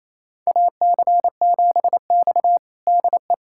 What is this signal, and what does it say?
Morse code, heard as a keyed tone, about 28 wpm, 700 Hz. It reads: AC7X DE
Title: USA Callsign DE Callsign - 4 Characters - 28wpm